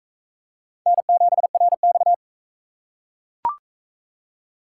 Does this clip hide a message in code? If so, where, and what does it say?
Yes: Morse code from 0.9 to 2.1 s, heard as a keyed tone, about 41 wpm, 700 Hz. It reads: N7RX